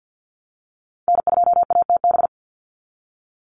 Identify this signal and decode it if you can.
Morse code, heard as a keyed tone, about 48 wpm, 700 Hz. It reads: N2ATB
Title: Call Signs Repeated 3x - 50wpm